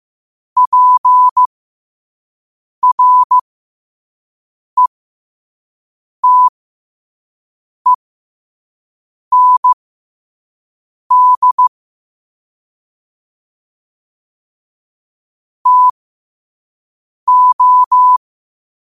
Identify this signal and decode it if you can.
Morse code, heard as a keyed tone, about 15 wpm, spaced as for 5 wpm, 1000 Hz. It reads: PRETEND TO
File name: qotdfiles / 5wpm